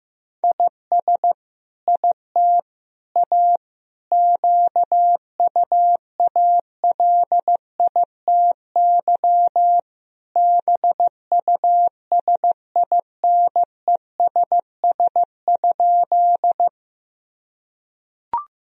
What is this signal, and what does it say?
Morse code, heard as a keyed tone, about 16 wpm, 700 Hz. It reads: IS IT A QUALITY BUSINESS?
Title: Sentences from Top 1000 Words - Review Part 4 - 15wpm